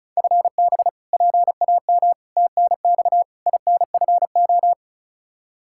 Morse code, heard as a keyed tone, about 36 wpm, 700 Hz. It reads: FB PAM TNX INFO
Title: Practice QSO - Exchange 3 - More Info - 35wpm